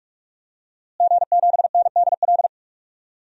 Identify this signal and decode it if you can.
Morse code, heard as a keyed tone, about 46 wpm, 700 Hz. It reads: G7NDL
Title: DXCC-100 Call Signs - 5 Characters - Repeated 3x - 45wpm